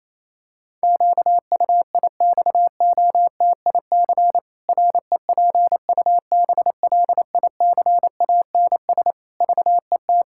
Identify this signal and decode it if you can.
Morse code, heard as a keyed tone, about 27 wpm, 700 Hz. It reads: QUIXOTIC REPUBLICANS VET
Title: Pangram Challenge v2 - 28wpm